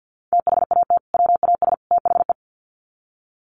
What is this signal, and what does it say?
Morse code, heard as a keyed tone, about 50 wpm, 700 Hz. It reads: THAT WAS THE
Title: Sets of 3 Words 50wpm